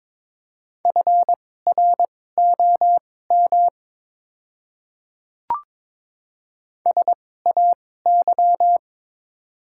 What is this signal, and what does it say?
Morse code, heard as a keyed tone, about 22 wpm, spaced as for 16 wpm, 700 Hz. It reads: FROM SAY
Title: 2x Character Spacing - Top 300 Words - 22wpm